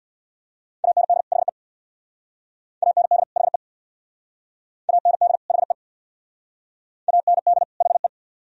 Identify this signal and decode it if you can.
Morse code, heard as a keyed tone, about 65 wpm, 700 Hz. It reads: AND HE AND HE AND HE AND HE